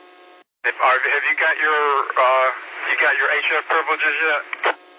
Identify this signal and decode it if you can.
Emergency Alert System (SAME) tones, heard as harsh buzzing bursts, with speech. It is heard from 1.2 to 4.4 s.